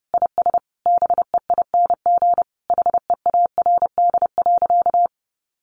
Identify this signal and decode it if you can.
Morse code, heard as a keyed tone, about 30 wpm, 700 Hz. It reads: IS BEING HEARD.
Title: Sentences from Top 1500 Words 30wpm